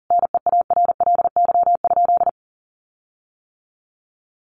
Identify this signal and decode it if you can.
Morse code, heard as a keyed tone, about 40 wpm, 700 Hz. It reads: NEARLY?
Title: Sentences from Top 1300 Words 40wpm